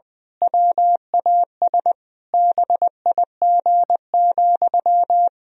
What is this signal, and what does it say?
Morse code, heard as a keyed tone, about 20 wpm, 700 Hz. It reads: WAS BIG,
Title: Sentences from Top 1000 Words - Review Part 3 - 20wpm